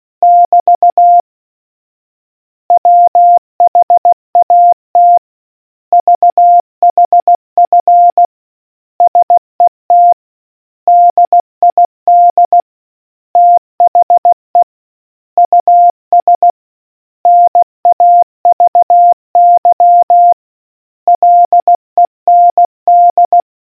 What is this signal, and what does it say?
Morse code, heard as a keyed tone, about 16 wpm, 700 Hz. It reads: <BT> WHAT VHF SET DID THE US NAVY LEND